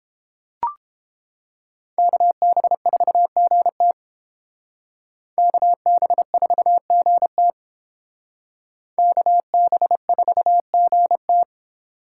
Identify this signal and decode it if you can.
Morse code, heard as a keyed tone, about 30 wpm, 700 Hz. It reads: KB4GT KB4GT KB4GT